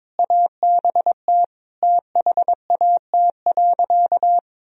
Morse code, heard as a keyed tone, about 22 wpm, 700 Hz. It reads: ABT THAT.
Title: Sentences from Top 200 Words and Abbreviations 22wpm